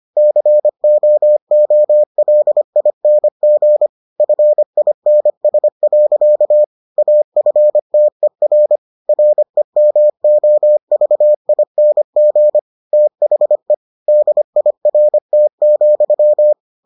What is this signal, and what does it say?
Morse code, heard as a keyed tone, about 25 wpm, 600 Hz. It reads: COOLING FINS. AFTER REMOVING THE DIRT,